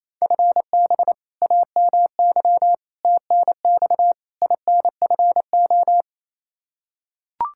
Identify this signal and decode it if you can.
Morse code, heard as a keyed tone, about 28 wpm, 700 Hz. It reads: FB AMY TNX INFO
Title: Practice QSO - Exchange 3 - More Info - 28wpm